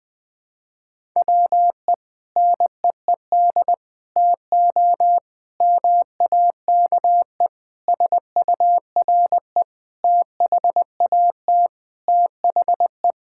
Morse code, heard as a keyed tone, about 20 wpm, 700 Hz. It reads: WE NEED TO MAKE SURE THAT THE
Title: Sentences with 5-Word Phrases - Spoken Beginning - 20wpm